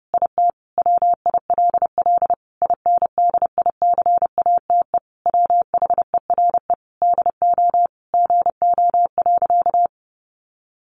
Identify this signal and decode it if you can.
Morse code, heard as a keyed tone, about 30 wpm, 700 Hz. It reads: IT WILL INDICATE WHERE DO GO.
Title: Sentences from Top 1300 Words 30wpm